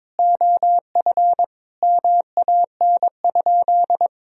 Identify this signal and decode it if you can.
Morse code, heard as a keyed tone, about 22 wpm, 700 Hz. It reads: OF MAN?
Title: Sentences from Top 1000 Words - Review Part 4 - 22wpm